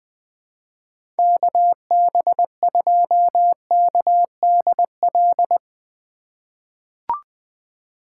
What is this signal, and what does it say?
Morse code, heard as a keyed tone, about 21 wpm, 700 Hz. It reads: KB2KDL
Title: USA Call Signs - 6 Characters - 2x Character Spacing - 20wpm